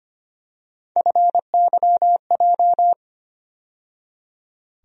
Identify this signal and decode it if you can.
Morse code, heard as a keyed tone, about 25 wpm, 700 Hz. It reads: FYJ
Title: USA Call Sign Postfixes - 3 Letters 25wpm